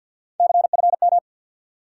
Morse code, heard as a keyed tone, about 53 wpm, 700 Hz. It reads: CFM
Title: Top 90 QSO Elements 50wpm